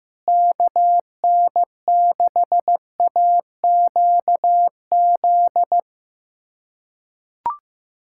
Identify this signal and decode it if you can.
Morse code, heard as a keyed tone, about 15 wpm, 700 Hz. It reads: KN6AQZ
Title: USA Callsign DE Callsign 15wpm